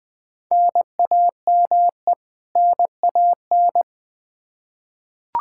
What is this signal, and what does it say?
Morse code, heard as a keyed tone, about 20 wpm, 700 Hz. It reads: NAME NAN
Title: Practice QSO - Exchange 1 - Beginning - 20wpm